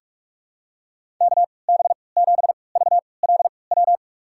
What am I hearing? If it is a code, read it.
Morse code, heard as a keyed tone, about 41 wpm, spaced as for 25 wpm, 700 Hz. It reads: KB7VLW
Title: USA Call Signs - 6 Characters - 3x Character Spacing - 45wpm